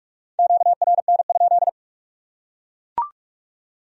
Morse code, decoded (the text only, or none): QRN?